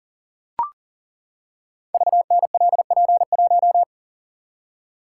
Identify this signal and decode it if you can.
Morse code, heard as a keyed tone, about 42 wpm, 690 Hz. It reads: VNLP1
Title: Code Groups - 5 Letter:Numbers - 40wpm